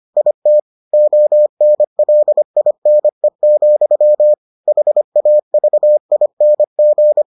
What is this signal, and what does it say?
Morse code, heard as a keyed tone, about 25 wpm, 600 Hz. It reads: IT ONLINE, HAVING